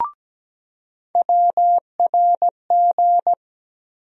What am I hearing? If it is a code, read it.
Morse code, heard as a keyed tone, about 17 wpm, 700 Hz. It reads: WRG
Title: USA Call Sign Postfixes - 3 Letters 17wpm